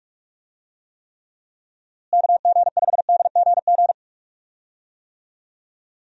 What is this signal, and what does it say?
Morse code, heard as a keyed tone, about 46 wpm, 700 Hz. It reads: KG5DGG
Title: USA Call Signs - 6 Characters - Repeated 3x - 45wpm